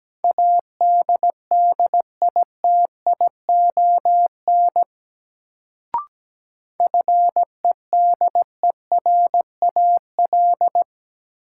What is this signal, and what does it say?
Morse code, heard as a keyed tone, about 17 wpm, 700 Hz. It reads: ADDITION FEDERAL
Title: Sets of 3 Words - New Word Drill - 17wpm